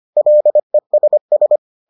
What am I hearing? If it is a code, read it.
Morse code, heard as a keyed tone, about 25 wpm, 600 Hz. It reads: LESS